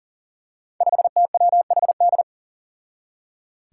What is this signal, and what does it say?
Morse code, heard as a keyed tone, about 40 wpm, 700 Hz. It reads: 5TWHD
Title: Five Challenging Letter-Number Combinations 40wpm